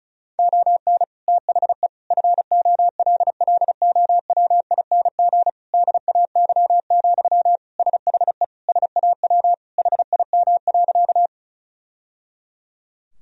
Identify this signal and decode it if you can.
Morse code, heard as a keyed tone, about 36 wpm, 700 Hz. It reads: ON THE FOLLOWING DAY, SHE SAW HIM.
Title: Sentences from Top 1000 Words - Review Part 4 - 35wpm